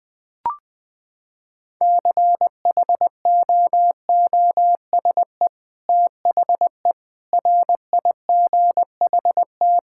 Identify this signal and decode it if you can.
Morse code, heard as a keyed tone, about 20 wpm, 700 Hz. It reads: CHOOSE THE RIGHT
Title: Sentences with 2-Word Phrases - 20wpm